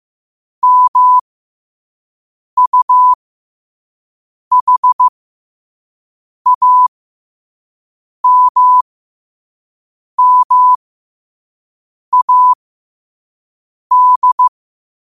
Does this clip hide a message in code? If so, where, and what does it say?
Yes: Morse code from 0.6 to 14.5 s, heard as a keyed tone, about 15 wpm, spaced as for 5 wpm, 1000 Hz. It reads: MUHAMMAD